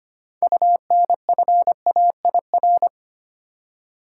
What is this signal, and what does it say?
Morse code, heard as a keyed tone, about 25 wpm, 700 Hz. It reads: UNFAIR
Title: Six Letter Words - 25wpm